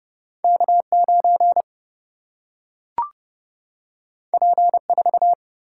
Morse code, heard as a keyed tone, about 30 wpm, 700 Hz. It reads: K 9 P 4
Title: Single-Letter Single-Number 30wpm